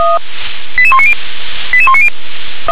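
Five-tone selective call (ZVEI), heard as quick runs of five tones, tones of about 70 ms. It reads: 88190 88190
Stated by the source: ZVEI five-tone call 88191 received by SDR.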